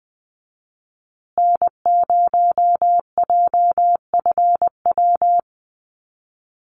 Morse code, heard as a keyed tone, about 20 wpm, 700 Hz. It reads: N0JFW
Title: USA Call Signs Repeated 2x 20wpm